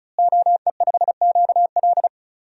Morse code, heard as a keyed tone, about 36 wpm, 700 Hz. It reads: OE5QL
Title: DXCC-100 Call Signs - 5 Characters - Repeated 2x - 35wpm